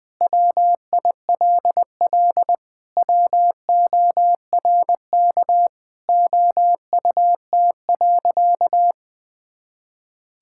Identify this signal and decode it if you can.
Morse code, heard as a keyed tone, about 21 wpm, 700 Hz. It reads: WILL WORK OUT.
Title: Sentences with 3-Word Phrases - 20wpm